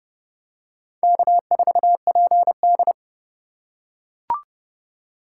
Morse code, heard as a keyed tone, about 31 wpm, 700 Hz. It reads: K4PD
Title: USA Call Signs - 4 Characters - 2x Character Spacing - 30wpm